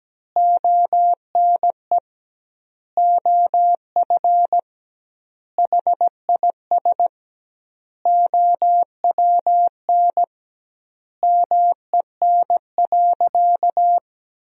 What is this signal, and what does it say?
Morse code, heard as a keyed tone, about 17 wpm, 700 Hz. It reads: ONE OF HIS OWN MEN.
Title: 2x Word-Spacing - Sentences from Top 300 Words - 17wpm